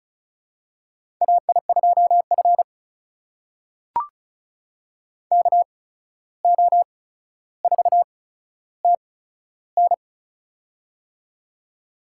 Morse code, heard as a keyed tone, about 36 wpm, spaced as for 14 wpm, 700 Hz. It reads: AI2F KO4TN
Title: USA Call Signs - 8x Character Spacing - 35wpm